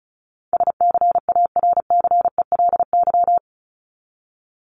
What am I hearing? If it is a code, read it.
Morse code, heard as a keyed tone, about 34 wpm, 700 Hz. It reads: SCARCELY
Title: LY Suffix - Top 100 Words with Base Words 35wpm